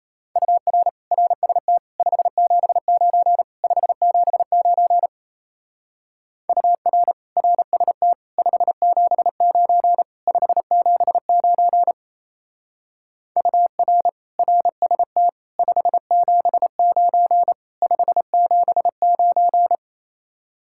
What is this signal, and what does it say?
Morse code, heard as a keyed tone, about 33 wpm, 700 Hz. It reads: UR RST 579 579 UR RST 579 579 UR RST 579 579
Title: Speed-Racing Common QSO Phrases - 28wpm